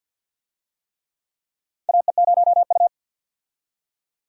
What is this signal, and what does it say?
Morse code, heard as a keyed tone, about 52 wpm, 700 Hz. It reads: AE0U
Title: USA Call Signs - 4 Characters - 50wpm